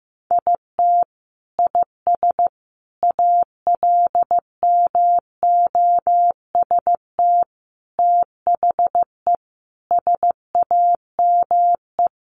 Morse code, heard as a keyed tone, about 15 wpm, 700 Hz. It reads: IT IS ALMOST THE SAME
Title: Sentences from Top 1400 Words 15wpm